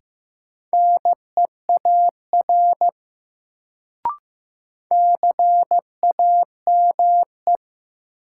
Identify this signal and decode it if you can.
Morse code, heard as a keyed tone, about 15 wpm, 700 Hz. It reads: NEAR CAME
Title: Sentences from Top 200 Words - New Word Drill - 15wpm